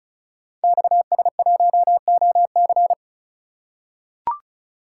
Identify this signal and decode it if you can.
Morse code, heard as a keyed tone, about 35 wpm, 700 Hz. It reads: XS1OC
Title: Code Groups - 5 Letter:Numbers - 35wpm